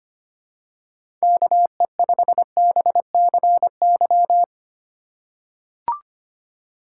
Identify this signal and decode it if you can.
Morse code, heard as a keyed tone, about 25 wpm, 700 Hz. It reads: KE5BCY
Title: K1USN SST QSO Practice - 25wpm